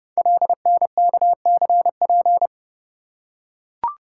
Morse code, heard as a keyed tone, about 30 wpm, 700 Hz. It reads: LNKCP
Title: Code Groups - 5 Letter:Numbers - 30wpm